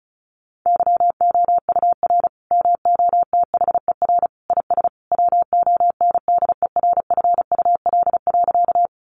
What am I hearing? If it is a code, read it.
Morse code, heard as a keyed tone, about 34 wpm, 700 Hz. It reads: YOUR MOTHER IS WONDERFUL.
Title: Sentences from Top 1100 Words 35wpm